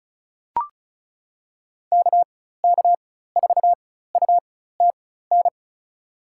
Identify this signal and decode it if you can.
Morse code, heard as a keyed tone, about 34 wpm, spaced as for 16 wpm, 700 Hz. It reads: KK4UTN
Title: USA Call Signs - 6 Characters - 4x Character Spacing - 35wpm